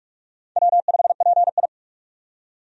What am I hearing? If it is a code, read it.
Morse code, heard as a keyed tone, about 44 wpm, 700 Hz. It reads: W5PI
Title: USA Call Signs - 4 Characters - 45wpm